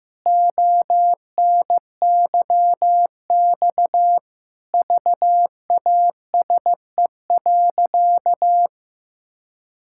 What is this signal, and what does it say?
Morse code, heard as a keyed tone, about 15 wpm, 700 Hz. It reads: ONYX VASE.
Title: Pangram Challenge v2 - 15wpm